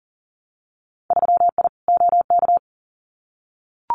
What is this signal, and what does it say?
Morse code, heard as a keyed tone, about 42 wpm, 700 Hz. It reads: 3I OK
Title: Winter Field Day Exchanges Encore 40wpm